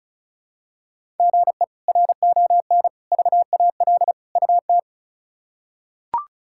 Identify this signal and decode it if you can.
Morse code, heard as a keyed tone, about 34 wpm, 700 Hz. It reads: GE RON VAL UT
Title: K1USN SST QSO Practice - 35wpm